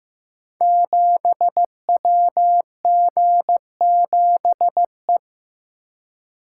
Morse code, heard as a keyed tone, about 15 wpm, 700 Hz. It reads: 7WG7E
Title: Code Groups - 5 Letter:Numbers - 15wpm